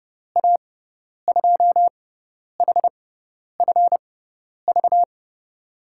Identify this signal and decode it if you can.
Morse code, heard as a keyed tone, about 30 wpm, spaced as for 10 wpm, 700 Hz. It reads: A2HFV
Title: USA Call Signs - 6 Characters - 6x Character Spacing - 30wpm